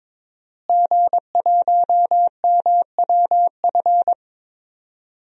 Morse code, heard as a keyed tone, about 21 wpm, 700 Hz. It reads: G1MWF
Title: DXCC-100 Call Signs - 5 Characters - Repeated 3x - 22wpm